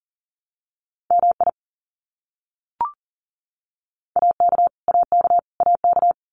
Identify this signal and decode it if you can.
Morse code, heard as a keyed tone, about 40 wpm, 700 Hz. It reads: MI AK AK AK
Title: US States Repeated 3x 40wpm